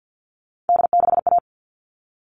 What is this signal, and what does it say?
Morse code, heard as a keyed tone, about 52 wpm, 700 Hz. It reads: D6A
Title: All Cut Numbers - 3-Digit 50wpm